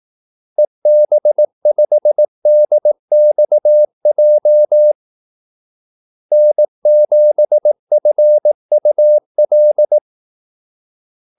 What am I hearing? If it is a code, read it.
Morse code, heard as a keyed tone, about 18 wpm, 600 Hz. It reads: EB5DXJ N7FUL
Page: Kaldesignaler 18 wpm